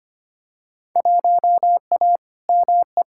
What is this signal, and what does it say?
Morse code, heard as a keyed tone, about 25 wpm, 700 Hz. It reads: 1A ME
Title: Field Day Exchanges Encore - 25wpm